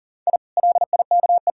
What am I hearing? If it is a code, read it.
Morse code, heard as a keyed tone, about 40 wpm, 700 Hz. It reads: I LIKE
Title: Sentences from Top 1000 Words - Review Part 6 - 40wpm